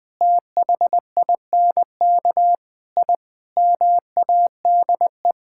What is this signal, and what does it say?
Morse code, heard as a keyed tone, about 20 wpm, 700 Hz. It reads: THINK I MADE
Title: Sentences from Top 1000 Words - Review Part 6 - 20wpm